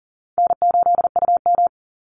Morse code, heard as a keyed tone, about 40 wpm, 700 Hz. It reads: N8UM
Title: Easy Call Signs Repeated 3x - 40wpm